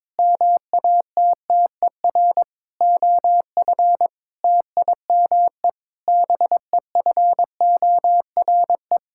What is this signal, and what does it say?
Morse code, heard as a keyed tone, about 22 wpm, 700 Hz. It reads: MATTER OF TIME BEFORE
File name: Sets of 9 Words - 22wpm